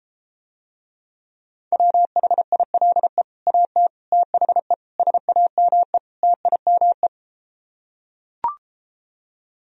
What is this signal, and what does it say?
Morse code, heard as a keyed tone, about 33 wpm, 700 Hz. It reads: WHILE AT THE SAME TIME